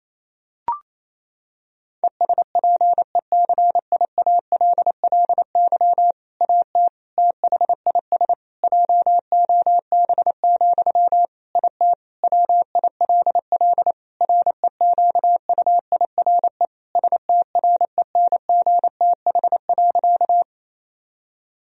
Morse code, heard as a keyed tone, about 27 wpm, 700 Hz. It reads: ESPECIALLY AT THIS JOB, IT WILL REQUIRE STRENGTH.
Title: Sentences from Top 1000 Words - Review Part 4 - 28wpm